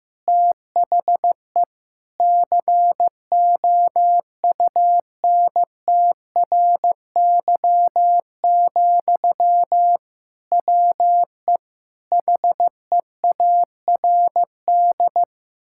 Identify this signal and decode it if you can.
Morse code, heard as a keyed tone, about 15 wpm, 700 Hz. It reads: THE COUNTRY, WE HEARD
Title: Sentences from Top 1000 Words - Review Part 5 - 15wpm